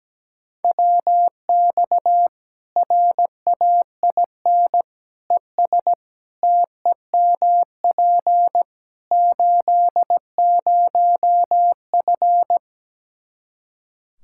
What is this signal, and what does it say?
Morse code, heard as a keyed tone, about 17 wpm, 700 Hz. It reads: WX RAIN ES TEMP 80F
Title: K1USN SST QSO Practice 17wpm